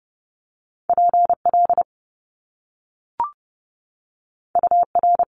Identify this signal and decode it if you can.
Morse code, heard as a keyed tone, about 30 wpm, 700 Hz. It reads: P L U R
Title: Top Two Letter Combinations in Words 30wpm